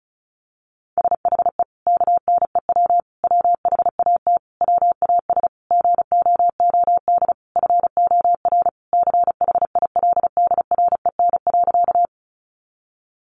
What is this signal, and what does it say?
Morse code, heard as a keyed tone, about 34 wpm, 700 Hz. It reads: SHE KNEW WHAT WAS GOOD FOR CHILDREN.
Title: Sentences from Top 400 Words 35wpm